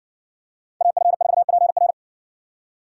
Morse code, heard as a keyed tone, about 60 wpm, 700 Hz. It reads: AF4PR